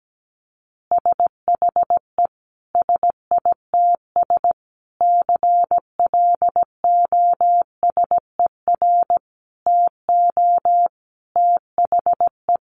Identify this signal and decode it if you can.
Morse code, heard as a keyed tone, about 17 wpm, 700 Hz. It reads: SHE SITS CLOSER TO THE
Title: Additional Easy Phrases 17wpm